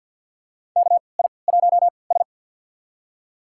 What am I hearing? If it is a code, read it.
Morse code, heard as a keyed tone, about 52 wpm, spaced as for 28 wpm, 700 Hz. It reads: KI1S
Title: USA Call Signs - 3x Character Spacing - 50wpm